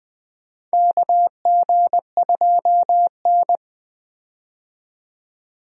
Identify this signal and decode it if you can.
Morse code, heard as a keyed tone, about 20 wpm, 700 Hz. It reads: KG2N
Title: USA Call Signs - 4 Characters - 20wpm